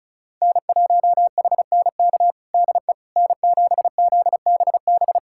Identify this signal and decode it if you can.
Morse code, heard as a keyed tone, about 36 wpm, 700 Hz. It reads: N1HNK DE N7ZBB
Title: USA Callsign DE Callsign - 5 Characters - 35wpm